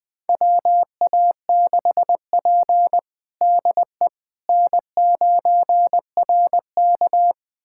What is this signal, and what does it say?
Morse code, heard as a keyed tone, about 21 wpm, 700 Hz. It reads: WA6P DE N9RK
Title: USA Callsign DE Callsign - 4 Characters - 20wpm